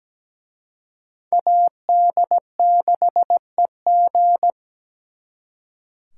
Morse code, heard as a keyed tone, about 20 wpm, 700 Hz. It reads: AD6EG
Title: Speed Racing - USA 2x2 Call Signs 17wpm